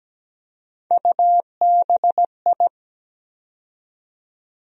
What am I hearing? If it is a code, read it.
Morse code, heard as a keyed tone, about 17 wpm, 700 Hz. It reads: UBI
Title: USA Call Sign Postfixes - 3 Letters 17wpm